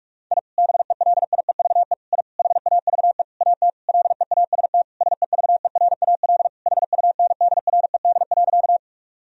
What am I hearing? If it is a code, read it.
Morse code, heard as a keyed tone, about 42 wpm, 700 Hz. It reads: I BELIEVE I HAVE AT LEAST SEVERAL HUNDRED.
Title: Sentences from Top 1000 Words - Review Part 2 - 45wpm